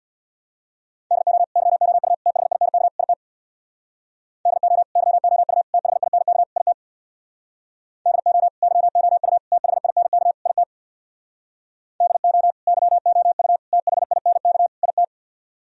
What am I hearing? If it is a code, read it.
Morse code, heard as a keyed tone, about 64 wpm, 700 Hz. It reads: DO YOU THINK IT DO YOU THINK IT DO YOU THINK IT DO YOU THINK IT